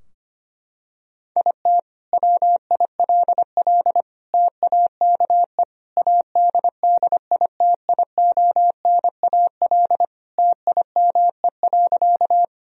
Morse code, heard as a keyed tone, about 25 wpm, 700 Hz. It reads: IT WILL TAKE ADDITIONAL TIME.
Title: Sentences from Top 1000 Words - Review Part 4 - 25wpm